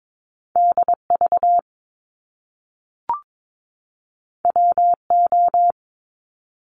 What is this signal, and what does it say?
Morse code, heard as a keyed tone, about 22 wpm, 700 Hz. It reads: DV WO
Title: Top Two Letter Combinations in Words 22wpm